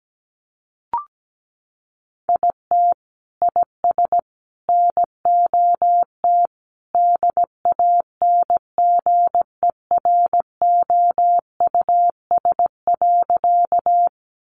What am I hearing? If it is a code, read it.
Morse code, heard as a keyed tone, about 17 wpm, 700 Hz. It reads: IT IS NOT DANGEROUS.
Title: Sentences from Top 1200 Words 17wpm